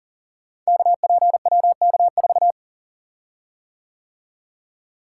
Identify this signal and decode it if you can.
Morse code, heard as a keyed tone, about 40 wpm, 700 Hz. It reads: KPWK4
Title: Code Groups - 5 Letter:Numbers - 40wpm